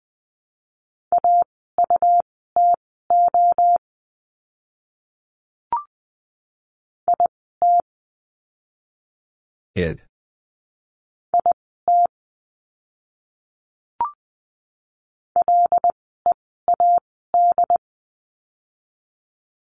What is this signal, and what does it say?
Morse code, heard as a keyed tone, about 20 wpm, spaced as for 15 wpm, 700 Hz. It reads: AUTO IT IT LEAD
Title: 101 Lesson 17f - Letter u - Words 20wpm